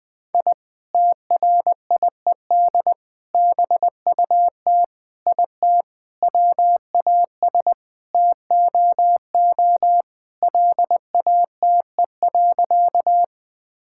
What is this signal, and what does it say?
Morse code, heard as a keyed tone, about 20 wpm, 700 Hz. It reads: I TRIED BUT IT WAS TOO LATE.
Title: Sentences with 3-Word Phrases - Spoken Beginning - 20wpm